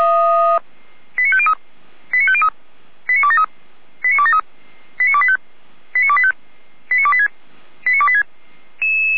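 Five-tone selective call (ZVEI), heard as quick runs of five tones, tones of about 70 ms. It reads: 89503 89503 89273 89273 89285 89285 89286 89286
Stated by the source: ZVEI five-tone call received on SDR.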